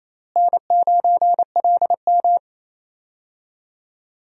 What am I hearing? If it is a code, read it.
Morse code, heard as a keyed tone, about 28 wpm, 700 Hz. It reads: N9LM
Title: USA Call Signs - 4 Characters - Repeated 3x - 28wpm